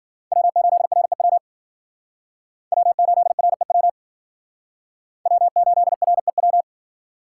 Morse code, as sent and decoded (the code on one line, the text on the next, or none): .-- ---.. .-. .-- / .-- ---.. .-. .-- / .-- ---.. .-. .--
W8REW W8REW W8REW